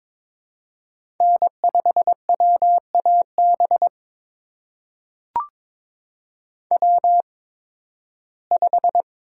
Morse code, heard as a keyed tone, about 22 wpm, spaced as for 14 wpm, 700 Hz. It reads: N5WAB W5
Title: USA Call Signs - 5 Characters - 8x Character Spacing - 22wpm